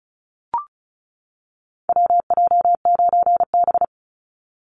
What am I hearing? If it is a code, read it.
Morse code, heard as a keyed tone, about 34 wpm, 700 Hz. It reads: WJ9B
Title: Call Signs - 35wpm